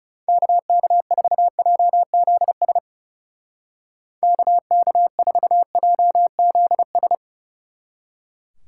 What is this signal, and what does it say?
Morse code, heard as a keyed tone, about 39 wpm, 700 Hz. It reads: KK4JZS KK4JZS